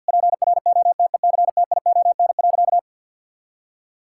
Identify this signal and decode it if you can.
Morse code, heard as a keyed tone, about 50 wpm, 700 Hz. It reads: PROTECTION.
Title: Sentences from Top 1600 Words 50wpm